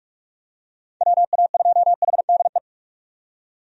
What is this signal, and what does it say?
Morse code, heard as a keyed tone, about 45 wpm, 700 Hz. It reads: WA2HDE
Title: USA Call Signs - 6 Characters - 45wpm